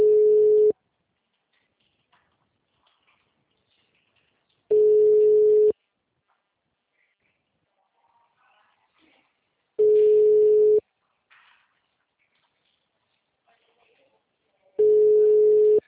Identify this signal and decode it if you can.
Ringback tone (Europe, 425 Hz), with speech.